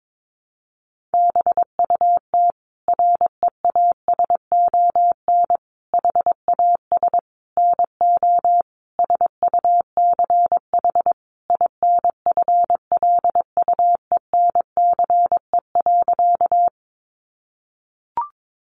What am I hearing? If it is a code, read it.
Morse code, heard as a keyed tone, about 22 wpm, 700 Hz. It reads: BUT REASON HAS NO SUCH INFLUENCE.
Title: Sentences from Top 1100 Words 22wpm